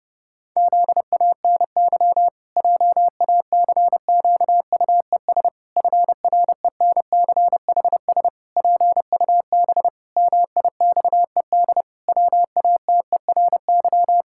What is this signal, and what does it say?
Morse code, heard as a keyed tone, about 32 wpm, 700 Hz. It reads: ZANY JACQUES FRENCHS PUB MIXED WATERY
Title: Pangram Challenge v2 - 30wpm